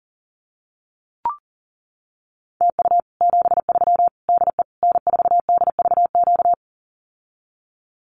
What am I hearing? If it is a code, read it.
Morse code, heard as a keyed tone, about 40 wpm, 700 Hz. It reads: TU 73 DE N4DVQ
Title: Common QSO Phrases Encore 40wpm